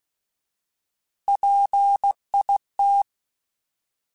Morse code, heard as a keyed tone, about 16 wpm, 800 Hz. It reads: PIT
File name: Alphabetical / Alphabetical mp3s / Ps @15